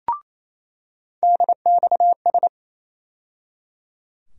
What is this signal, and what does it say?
Morse code, heard as a keyed tone, about 28 wpm, 700 Hz. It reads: DXS
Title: USA Call Sign Postfixes - 3 Letters 28wpm